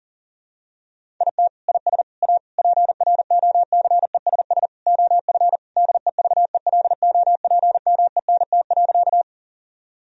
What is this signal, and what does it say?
Morse code, heard as a keyed tone, about 40 wpm, 700 Hz. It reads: IT IS A PROCESS OF DEVELOPMENT.
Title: Sentences from Top 1000 Words - Review Part 4 - 40wpm